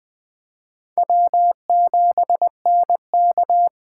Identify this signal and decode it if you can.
Morse code, heard as a keyed tone, about 21 wpm, 700 Hz. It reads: W7NK